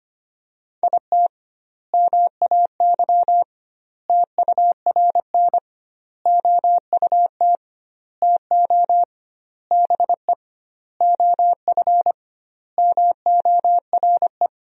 Morse code, heard as a keyed tone, about 25 wpm, 700 Hz. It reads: IT MAY TURN OUT TO BE OF MORE
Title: 2x Word-Spacing - Sentences from Top 400 Words - 25wpm